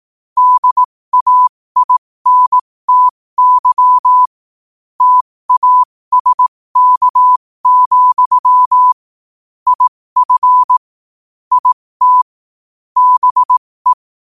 Morse code, heard as a keyed tone, about 18 wpm, spaced as for 15 wpm, 1000 Hz. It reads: DAINTY TASK, IF IT BE
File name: qotdfiles / 15wpm